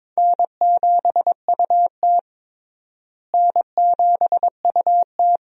Morse code, heard as a keyed tone, about 21 wpm, 700 Hz. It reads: N7UT N7UT
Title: USA Call Signs - 4 Characters - Repeated 2x - 22wpm